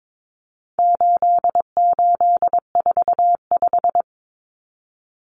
Morse code, heard as a keyed tone, about 22 wpm, 700 Hz. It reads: 8845
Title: Numbers - 4-Digit 22wpm